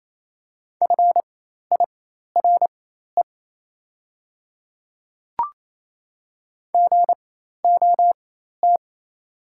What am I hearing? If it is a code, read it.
Morse code, heard as a keyed tone, about 28 wpm, spaced as for 13 wpm, 700 Hz. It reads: FIRE GOT
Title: 4x Character Spacing - Top 500 Words - 28wpm